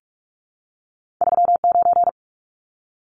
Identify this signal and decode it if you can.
Morse code, heard as a keyed tone, about 46 wpm, 700 Hz. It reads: 39
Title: Numbers - 2-Digit 45wpm